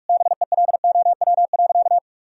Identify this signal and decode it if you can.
Morse code, heard as a keyed tone, about 41 wpm, 700 Hz. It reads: BELOW.
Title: Sentences from Top 1000 Words - Review Part 5 - 45wpm